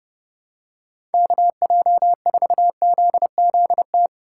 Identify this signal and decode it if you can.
Morse code, heard as a keyed tone, about 30 wpm, 700 Hz. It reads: KJ4ZZT
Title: USA Call Signs - 6 Characters - 3x Character Spacing - 30wpm